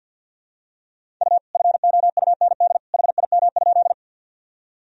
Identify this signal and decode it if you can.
Morse code, heard as a keyed tone, about 51 wpm, 700 Hz. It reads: U FOUND HIM?
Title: Sentences from Top 300 Words and Abbreviations 50wpm